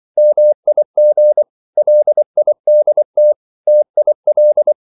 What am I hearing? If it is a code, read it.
Morse code, heard as a keyed tone, about 24 wpm, 600 Hz. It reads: MIG LIDT TIL
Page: Ferie 24 wpm